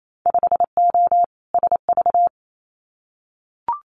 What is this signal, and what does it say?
Morse code, heard as a keyed tone, about 28 wpm, 700 Hz. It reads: 5O SV
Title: Winter Field Day Exchanges Encore 28wpm